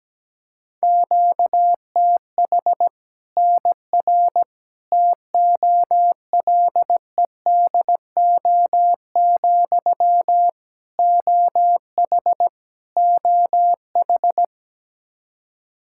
Morse code, heard as a keyed tone, about 17 wpm, 700 Hz. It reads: QTH NR TOLEDO, OH OH
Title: Practice QSO - Exchange 1 - Beginning - 17wpm